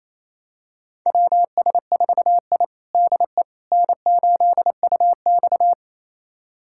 Morse code, heard as a keyed tone, about 28 wpm, 700 Hz. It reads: WS4I DE N8UX
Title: USA Callsign DE Callsign - 4 Characters - 28wpm